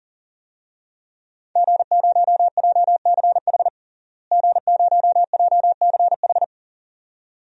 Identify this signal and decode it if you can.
Morse code, heard as a keyed tone, about 40 wpm, 700 Hz. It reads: G0JCH G0JCH
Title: DXCC-100 Call Signs - 5 Characters - Repeated 2x - 40wpm